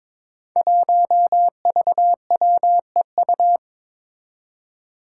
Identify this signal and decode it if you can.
Morse code, heard as a keyed tone, about 22 wpm, 700 Hz. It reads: 1VWEU
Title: Five Challenging Letter-Number Combinations 22wpm